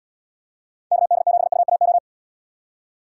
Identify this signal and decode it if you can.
Morse code, heard as a keyed tone, about 72 wpm, 700 Hz. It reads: KD7UAO